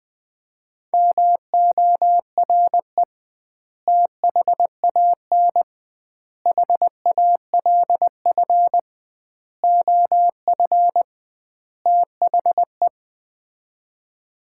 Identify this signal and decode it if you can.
Morse code, heard as a keyed tone, about 19 wpm, 700 Hz. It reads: MORE THAN HALF OF THE
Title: 2x Word-Spacing - Sets of 5 Words - 20wpm